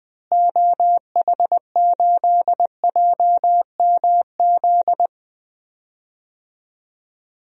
Morse code, heard as a keyed tone, about 20 wpm, 700 Hz. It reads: OH8JMZ
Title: DXCC-100 Call Signs - 6 Characters - Repeated 2x - 20wpm